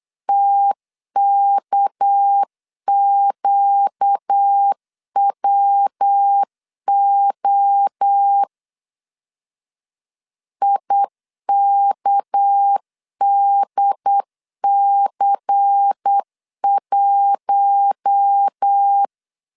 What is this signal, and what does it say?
Morse code, heard as a keyed tone, about 8 wpm, 800 Hz. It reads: TKQWO IKDC1
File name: LEZIONI / 8 YJ45 / 8 YJ4 8wpm 3246